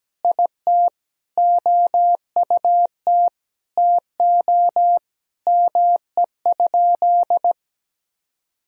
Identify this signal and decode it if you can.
Morse code, heard as a keyed tone, about 17 wpm, 700 Hz. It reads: IT OUT TO ME?
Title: Sentences from Top 200 Words and Abbreviations 17wpm